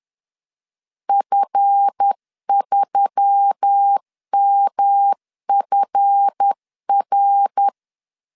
Morse code, heard as a keyed tone, about 10 wpm, 800 Hz. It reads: F3MFR
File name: LEZIONI / 7 GF9Z / 7 GF 10wpm 2514